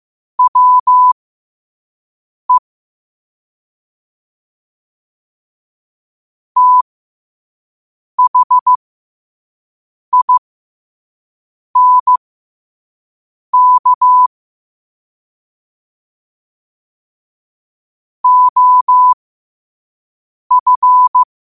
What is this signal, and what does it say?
Morse code, heard as a keyed tone, about 15 wpm, spaced as for 5 wpm, 1000 Hz. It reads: WE THINK OF
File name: qotdfiles / 5wpm